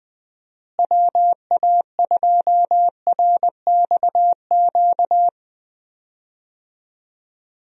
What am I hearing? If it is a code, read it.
Morse code, heard as a keyed tone, about 20 wpm, 700 Hz. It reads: WA2RXQ
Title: USA Call Signs - 6 Characters - Repeated 2x - 20wpm